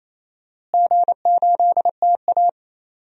Morse code, heard as a keyed tone, about 28 wpm, 700 Hz. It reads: G8TA
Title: DXCC-100 Call Signs - 4 Characters - Repeated 3x - 28wpm